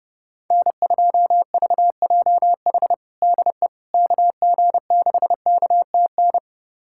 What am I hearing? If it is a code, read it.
Morse code, heard as a keyed tone, about 28 wpm, 700 Hz. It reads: N2VJH DE KG6KTN
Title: USA Callsign DE Callsign 30wpm